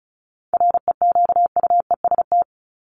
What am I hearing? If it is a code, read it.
Morse code, heard as a keyed tone, about 34 wpm, 700 Hz. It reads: REQUEST
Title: Top 1751-2000 Words 35wpm